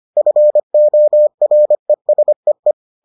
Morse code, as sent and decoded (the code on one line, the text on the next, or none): ..-. --- .-. ... .
FORESEE